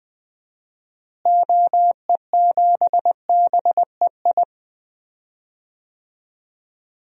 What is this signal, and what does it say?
Morse code, heard as a keyed tone, about 20 wpm, 700 Hz. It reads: OE7BEI
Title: DXCC-100 Call Signs - 6 Characters - Repeated 2x - 20wpm